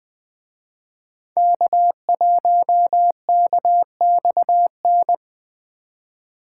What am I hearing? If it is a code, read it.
Morse code, heard as a keyed tone, about 25 wpm, 700 Hz. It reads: K1KXN
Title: Speed Racing - USA 1x3 Call Signs 20wpm